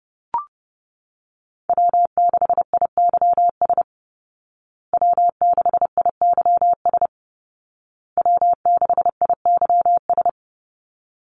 Morse code, heard as a keyed tone, about 30 wpm, 700 Hz. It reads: W6IYS W6IYS W6IYS
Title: Call Signs Repeated 2x - 30wpm